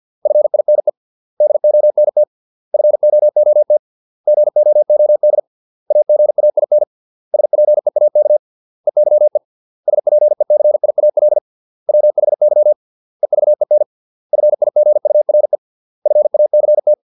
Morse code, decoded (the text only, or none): FINE DONT FOOT GOOD AGAIN SPEAK EYE SPECIAL WHY EVEN FIGURE FACT